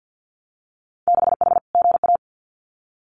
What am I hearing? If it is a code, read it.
Morse code, heard as a keyed tone, about 50 wpm, 700 Hz. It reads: 6H GA
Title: Winter Field Day Exchanges Encore 50wpm